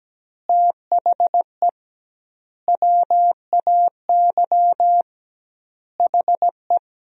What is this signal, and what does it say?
Morse code, heard as a keyed tone, about 17 wpm, 700 Hz. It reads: THE WAY HE
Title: 2x Word-Spacing - Sentences from Top 200 Words - 17wpm